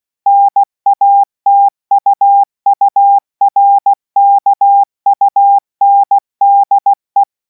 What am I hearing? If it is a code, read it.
Morse code, heard as a keyed tone, about 16 wpm, 800 Hz. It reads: NATUURKUNDE